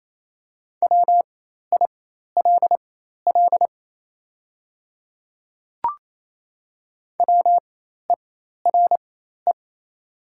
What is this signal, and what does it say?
Morse code, heard as a keyed tone, about 28 wpm, spaced as for 13 wpm, 700 Hz. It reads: WILL WERE
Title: 4x Character Spacing - Top 100 Words - 28wpm